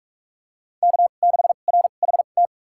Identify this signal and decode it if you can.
Morse code, heard as a keyed tone, about 46 wpm, spaced as for 32 wpm, 700 Hz. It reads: K6RHT
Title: USA Call Signs - 2x Character Spacing - 45wpm